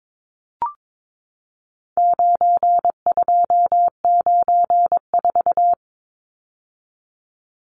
Morse code, heard as a keyed tone, about 22 wpm, 700 Hz. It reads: 9294
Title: Numbers - 4-Digit 22wpm